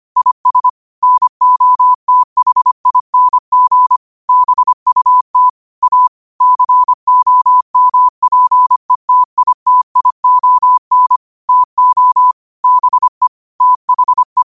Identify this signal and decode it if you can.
Morse code, heard as a keyed tone, about 25 wpm, 1000 Hz. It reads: IS NOTHING BUT A COMPETITION TO BE THE
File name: qotdfiles / 25wpm